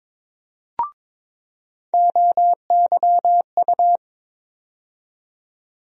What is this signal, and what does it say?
Morse code, heard as a keyed tone, about 22 wpm, 700 Hz. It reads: OYU
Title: USA Call Sign Postfixes - 3 Letters 22wpm